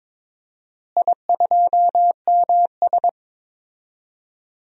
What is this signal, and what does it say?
Morse code, heard as a keyed tone, about 22 wpm, 700 Hz. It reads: I2MS
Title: DXCC-100 Call Signs - 4 Characters - Repeated 3x - 22wpm